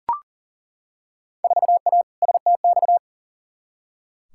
Morse code, heard as a keyed tone, about 40 wpm, 700 Hz. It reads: 4A STX
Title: Field Day Exchanges Encore - 40wpm